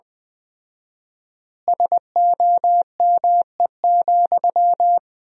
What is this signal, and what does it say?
Morse code, heard as a keyed tone, about 20 wpm, 700 Hz. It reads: SOME,
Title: 4x Word-Spacing - Sentences from Top 200 Words - 20wpm